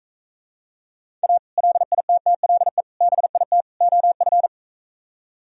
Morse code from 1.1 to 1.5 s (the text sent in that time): A